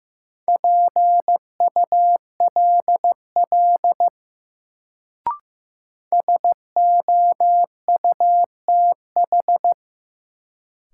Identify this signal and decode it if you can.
Morse code, heard as a keyed tone, about 16 wpm, 700 Hz. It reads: PULL SOUTH
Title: Sentences from Top 400 Words - New Word Drill - 15wpm